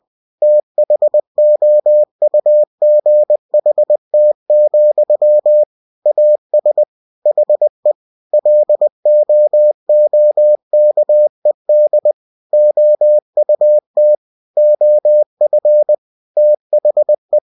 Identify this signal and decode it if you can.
Morse code, heard as a keyed tone, about 20 wpm, 600 Hz. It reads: THOUGHT, AS HE LOOKED OUT OF THE
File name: cw-books / winnie-the-pooh / winnie-20wpm-page-0090